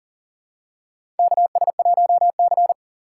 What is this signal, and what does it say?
Morse code, heard as a keyed tone, about 40 wpm, 700 Hz. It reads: KS1C
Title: USA Call Signs - 4 Characters - 2x Character Spacing - 40wpm